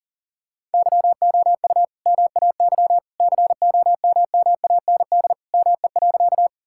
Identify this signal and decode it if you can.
Morse code, heard as a keyed tone, about 40 wpm, 700 Hz. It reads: YOU MAY COMMAND ME.
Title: Sentences from Top 1600 Words 40wpm